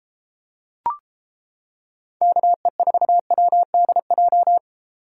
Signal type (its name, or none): Morse code